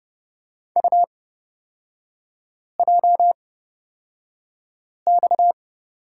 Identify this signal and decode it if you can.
Morse code, heard as a keyed tone, about 29 wpm, spaced as for 5 wpm, 690 Hz. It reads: UJX
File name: Lesson 37e - Letter x - Three Characters 30wpm